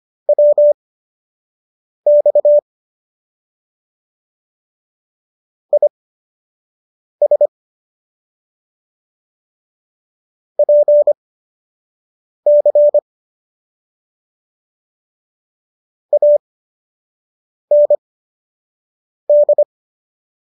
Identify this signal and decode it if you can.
Morse code, heard as a keyed tone, about 25 wpm, spaced as for 6 wpm, 590 Hz. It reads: WX IS PC AND